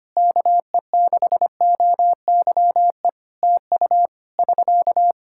Morse code, heard as a keyed tone, about 25 wpm, 700 Hz. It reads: KE6OYE TU <SK>
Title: Practice QSO - Exchange 4 - Ending - 25wpm